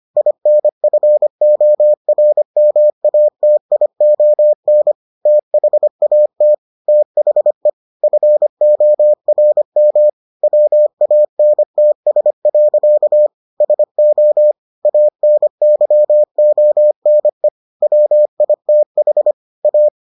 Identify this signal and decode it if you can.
Morse code, heard as a keyed tone, about 25 wpm, 600 Hz. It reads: INFORMATION THAT THE FORM WANTS. SO ANYONE WITH A